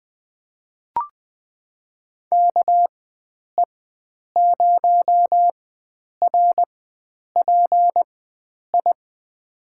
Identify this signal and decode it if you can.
Morse code, heard as a keyed tone, about 21 wpm, spaced as for 9 wpm, 700 Hz. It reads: KE0RPI
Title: USA Call Signs - 6 Characters - 4x Character Spacing - 20wpm